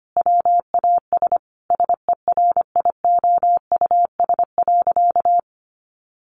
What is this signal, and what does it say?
Morse code, heard as a keyed tone, about 25 wpm, 700 Hz. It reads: WAS SERIOUS.
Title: Sentences from Top 1100 Words 25wpm